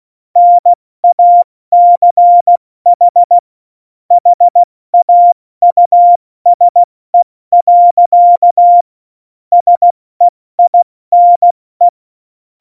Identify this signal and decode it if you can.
Morse code, heard as a keyed tone, about 16 wpm, 700 Hz. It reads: NACH HAUSE. SEINE
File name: Morsekurs / Witze / Witz 29. Oktober 2019, 70 F80